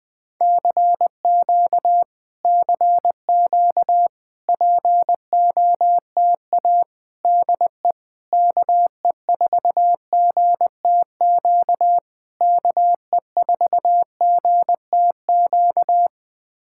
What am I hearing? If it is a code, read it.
Morse code, heard as a keyed tone, about 20 wpm, 700 Hz. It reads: CQ CQ POTA DE KE4GTQ KE4GTQ
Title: POTA QSO Practice - 20wpm